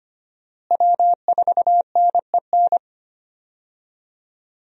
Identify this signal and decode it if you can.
Morse code, heard as a keyed tone, about 25 wpm, 700 Hz. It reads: W4NEN
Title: USA Call Signs - 5 Characters - Repeated 2x - 25wpm